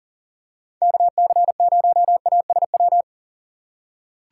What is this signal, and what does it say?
Morse code, heard as a keyed tone, about 40 wpm, 700 Hz. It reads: KC0ASW
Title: USA Call Signs - 6 Characters - 3x Character Spacing - 40wpm